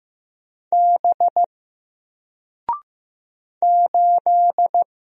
Morse code, heard as a keyed tone, about 15 wpm, 700 Hz. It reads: B 8
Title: Single Character (Letter:Number:Punctuation:Prosign) Alt 15wpm